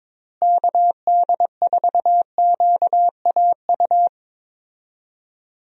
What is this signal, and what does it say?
Morse code, heard as a keyed tone, about 22 wpm, 700 Hz. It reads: KD4QAU
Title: USA Call Signs - 6 Characters - Repeated 3x - 22wpm